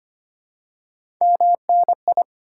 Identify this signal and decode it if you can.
Morse code, heard as a keyed tone, about 25 wpm, 700 Hz. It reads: MNI
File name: Top 90 QSO Elements - 4x Character Spacing - 25wpm